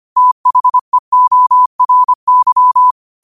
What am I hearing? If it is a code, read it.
Morse code, heard as a keyed tone, about 25 wpm, 1000 Hz. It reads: THEORY
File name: 25wpm